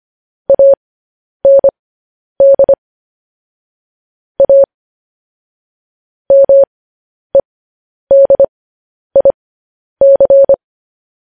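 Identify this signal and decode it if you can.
Morse code, heard as a keyed tone, about 25 wpm, spaced as for 10 wpm, 550 Hz. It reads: AND A MEDIC
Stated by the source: Morse code news headlines.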